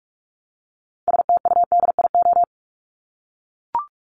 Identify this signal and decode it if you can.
Morse code, heard as a keyed tone, about 44 wpm, 700 Hz. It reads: STUDIO
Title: Top 1001-1250 Words 45wpm